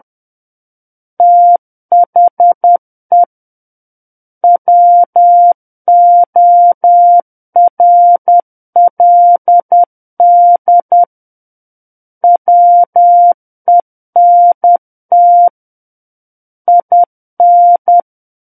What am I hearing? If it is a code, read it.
Morse code, heard as a keyed tone, about 10 wpm, 700 Hz. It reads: THE WORLD WENT IN
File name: Book One / 01-Chapter 1 - 10 WPM